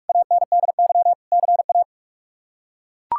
Morse code, heard as a keyed tone, about 41 wpm, 700 Hz. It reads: ANDY CA
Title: K1USN SST QSO Practice - 45wpm